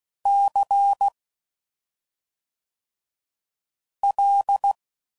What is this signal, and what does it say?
Morse code, heard as a keyed tone, about 16 wpm, 800 Hz. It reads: C L
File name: Suffixes / Special ING's / SlowToFaster ING's